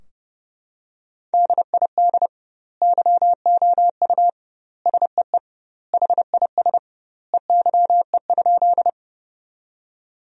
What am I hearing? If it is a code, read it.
Morse code, heard as a keyed tone, about 30 wpm, 700 Hz. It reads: DID YOU SEE HIS EYE?
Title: 2x Word-Spacing - Sentences from Top 300 Words - 30wpm